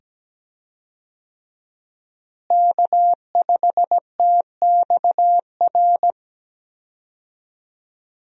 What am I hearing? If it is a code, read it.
Morse code, heard as a keyed tone, about 17 wpm, 700 Hz. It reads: K5TXR
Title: USA Call Signs - 5 Characters - Repeated 2x - 17wpm